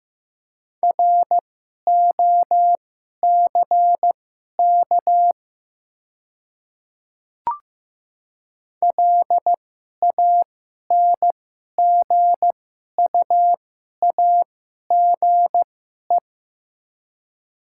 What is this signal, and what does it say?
Morse code, heard as a keyed tone, about 16 wpm, spaced as for 11 wpm, 700 Hz. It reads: ROCK LANGUAGE
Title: 2x Character Spacing - Top 500 Words - 15wpm